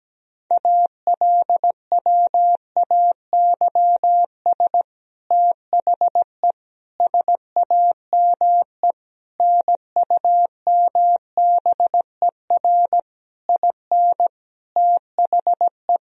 Morse code, heard as a keyed tone, about 17 wpm, 700 Hz. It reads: ALWAYS THE SAME NUMBER IN THE
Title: Sentences from Top 1000 Words - Review Part 1 - 17wpm